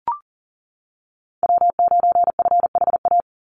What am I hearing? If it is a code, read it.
Morse code, heard as a keyed tone, about 40 wpm, 700 Hz. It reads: W9FHA
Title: NA SOTA Chaser Callsigns 40wpm